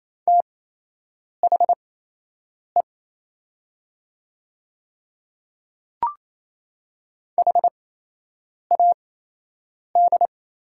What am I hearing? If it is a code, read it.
Morse code, heard as a keyed tone, about 27 wpm, spaced as for 8 wpm, 700 Hz. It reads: THE HAD
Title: 8x Character Spacing - Top 100 Words - 28wpm